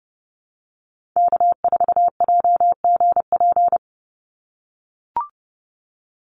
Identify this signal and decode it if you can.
Morse code, heard as a keyed tone, about 30 wpm, 700 Hz. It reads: K4JGP
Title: USA Call Signs 30wpm